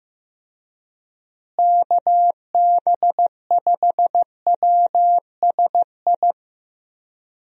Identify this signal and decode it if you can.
Morse code, heard as a keyed tone, about 15 wpm, 700 Hz. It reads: KB5WSI
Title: USA Call Signs - 6 Characters - Repeated 3x - 15wpm